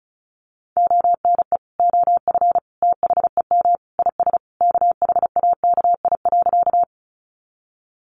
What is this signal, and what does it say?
Morse code, heard as a keyed tone, about 34 wpm, 700 Hz. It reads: ONE OF THEM IS KHAKI.
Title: Additional Easy Phrases 35wpm